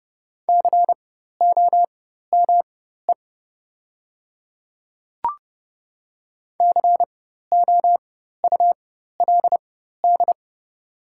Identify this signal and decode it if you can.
Morse code, heard as a keyed tone, about 30 wpm, spaced as for 14 wpm, 700 Hz. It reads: COME COULD
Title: 4x Character Spacing - Top 200 Words - 30wpm